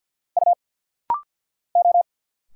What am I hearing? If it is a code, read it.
Morse code, heard as a keyed tone, about 50 wpm, 690 Hz. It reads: U O
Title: Single Letters - Rapid-Fire 50wpm